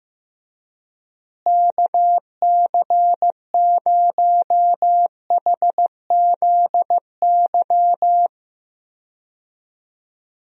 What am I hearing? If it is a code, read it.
Morse code, heard as a keyed tone, about 15 wpm, 700 Hz. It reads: KC0HZY
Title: USA Call Signs - 6 Characters - Repeated 2x - 15wpm